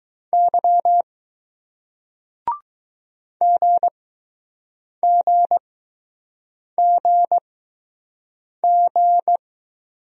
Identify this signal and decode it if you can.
Morse code, heard as a keyed tone, about 20 wpm, spaced as for 6 wpm, 700 Hz. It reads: Y GGGG